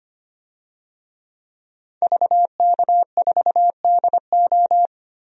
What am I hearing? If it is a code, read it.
Morse code, heard as a keyed tone, about 25 wpm, 700 Hz. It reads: VK4DO
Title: DXCC-100 Call Signs - 5 Characters - 25wpm